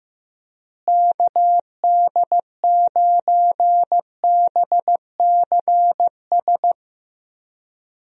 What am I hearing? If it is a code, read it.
Morse code, heard as a keyed tone, about 15 wpm, 700 Hz. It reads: KD9BCS
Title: USA Call Signs - 6 Characters - Repeated 3x - 15wpm